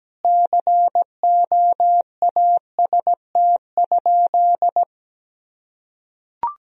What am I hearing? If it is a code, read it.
Morse code, heard as a keyed tone, about 17 wpm, 700 Hz. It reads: COAST?
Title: Sentences from Top 1000 Words - Review Part 4 - 17wpm